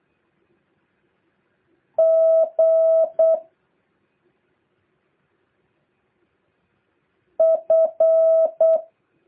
Morse code, heard as a keyed tone, about 8 wpm, 650 Hz. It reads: G F
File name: jh8cbh / 2023cw